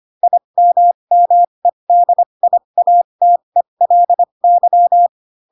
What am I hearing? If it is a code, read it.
Morse code, heard as a keyed tone, about 25 wpm, spaced as for 22 wpm, 700 Hz. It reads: IMMEDIATELY
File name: Pinocchio / Pinocchio ch2 18 25 ik1hge